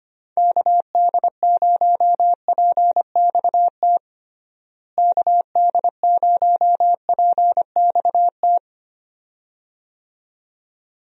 Morse code, heard as a keyed tone, about 25 wpm, 700 Hz. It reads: KD0PXT KD0PXT
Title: USA Call Signs - 6 Characters - Repeated 2x - 25wpm